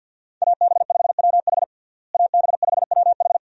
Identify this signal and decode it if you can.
Morse code, heard as a keyed tone, about 51 wpm, 700 Hz. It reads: AB5WH AB5WH
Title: USA Call Signs - 5 Characters - Repeated 2x - 50wpm